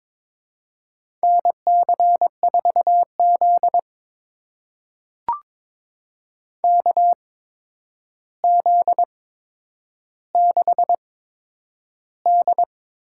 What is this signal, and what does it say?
Morse code, heard as a keyed tone, about 22 wpm, spaced as for 9 wpm, 700 Hz. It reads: NC4Z KZ6D
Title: USA Call Signs - 4 Characters - 8x Character Spacing - 22wpm